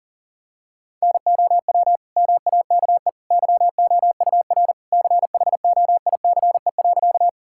Morse code, heard as a keyed tone, about 40 wpm, 700 Hz. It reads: NOW MAKE YOUR CHOICE.
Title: Sentences from Top 1000 Words - Review Part 6 - 40wpm